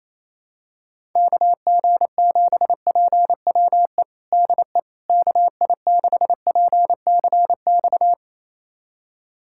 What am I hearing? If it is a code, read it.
Morse code, heard as a keyed tone, about 28 wpm, 700 Hz. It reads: KG7PWE DE KI6PCX
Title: USA Callsign DE Callsign - 6 Characters - 28wpm